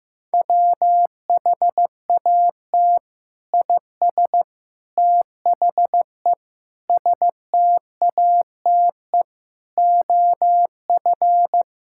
Morse code, heard as a keyed tone, about 16 wpm, 700 Hz. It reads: WHAT IS THE STATE OF
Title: Sentences from Top 1000 Words - Review Part 5 - 15wpm